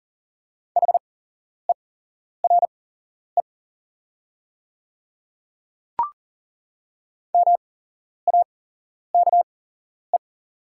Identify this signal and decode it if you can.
Morse code, heard as a keyed tone, about 38 wpm, spaced as for 11 wpm, 700 Hz. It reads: HERE MAKE
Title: 8x Character Spacing - Top 200 Words - 40wpm